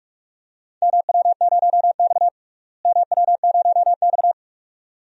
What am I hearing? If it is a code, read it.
Morse code, heard as a keyed tone, about 46 wpm, 700 Hz. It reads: MW0X MW0X
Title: DXCC-100 Call Signs - 4 Characters - Repeated 2x - 45wpm